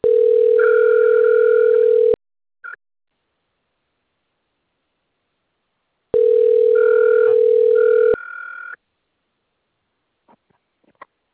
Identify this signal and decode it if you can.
Ringback tone (North America), 440 + 480 Hz.